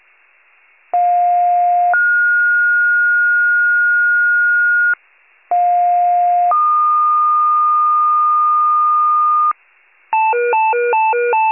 Two-tone page, heard as a one-second tone, then a three-second tone, about 710 then 1430 Hz, about 710 then 1220 Hz, with noise. It is heard from 0.9 to 9.5 s.
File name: wcalert